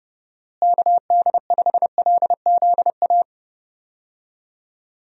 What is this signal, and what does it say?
Morse code, heard as a keyed tone, about 32 wpm, 700 Hz. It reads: KD5LZA
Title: USA Call Signs - 6 Characters - Repeated 3x - 30wpm